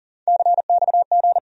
Morse code, heard as a keyed tone, about 40 wpm, 700 Hz. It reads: CXG
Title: USA Call Sign Postfixes - 3 Letters 40wpm